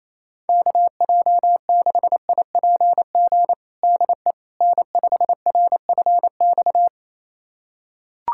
Morse code, heard as a keyed tone, about 28 wpm, 700 Hz. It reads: KJ6IPG DE N5RFX
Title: USA Callsign DE Callsign 28wpm